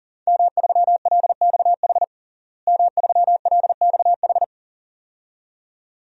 Morse code, heard as a keyed tone, about 40 wpm, 700 Hz. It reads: M3LXH M3LXH
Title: DXCC-100 Call Signs - 5 Characters - Repeated 2x - 40wpm